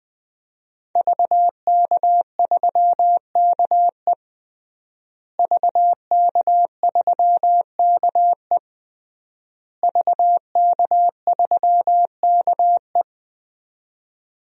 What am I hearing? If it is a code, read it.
Morse code, heard as a keyed tone, about 20 wpm, 700 Hz. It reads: VK3KE VK3KE VK3KE
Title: DXCC-100 Call Signs - 5 Characters - Repeated 3x - 20wpm